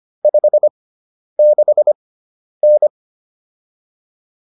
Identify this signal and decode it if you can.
Morse code, heard as a keyed tone, about 25 wpm, spaced as for 10 wpm, 600 Hz. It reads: 56N
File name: QSO303 10